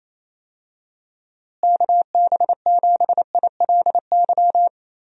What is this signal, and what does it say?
Morse code, heard as a keyed tone, about 28 wpm, 700 Hz. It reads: KB7ILY
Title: USA Call Signs - 6 Characters - Repeated 3x - 28wpm